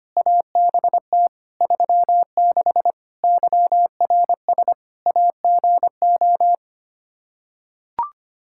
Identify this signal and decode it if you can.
Morse code, heard as a keyed tone, about 25 wpm, 700 Hz. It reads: ABT 36 YRS AGO
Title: Practice QSO - Exchange 3 - More Info - 25wpm